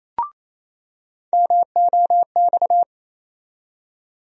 Morse code, heard as a keyed tone, about 28 wpm, 700 Hz. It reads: MOX
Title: USA Call Sign Postfixes - 3 Letters 28wpm